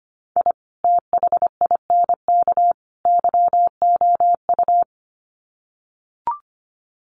Morse code, heard as a keyed tone, about 25 wpm, 700 Hz. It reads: I THINK YOU
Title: Sets of 3 Words 25wpm